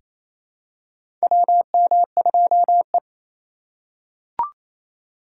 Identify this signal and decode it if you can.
Morse code, heard as a keyed tone, about 28 wpm, 700 Hz. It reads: WM2E
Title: USA Call Signs - 3x Character Spacing - 28wpm